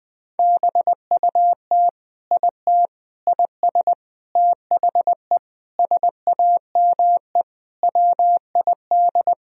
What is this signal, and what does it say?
Morse code, heard as a keyed tone, about 20 wpm, 700 Hz. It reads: BUT IT IS THE SAME WID
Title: Sentences from Top 300 Words and Abbreviations 20wpm